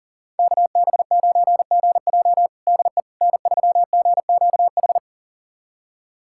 Morse code, heard as a keyed tone, about 40 wpm, 700 Hz. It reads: KB9GJ DE N3GQH
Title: USA Callsign DE Callsign - 5 Characters - 40wpm